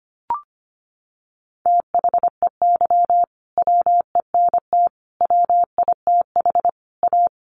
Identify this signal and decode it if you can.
Morse code, heard as a keyed tone, about 25 wpm, 700 Hz. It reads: THEY WENT WITH A
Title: Sentences from Top 800 Words 25wpm